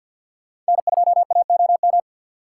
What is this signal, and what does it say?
Morse code, heard as a keyed tone, about 64 wpm, 700 Hz. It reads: N2AOM